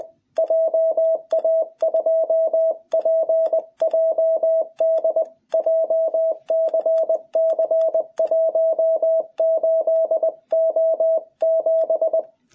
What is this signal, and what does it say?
Morse code, heard as a keyed tone, about 20 wpm, 660 Hz. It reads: JA2PJD JCC 18O7
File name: jh8cbh / 2021cw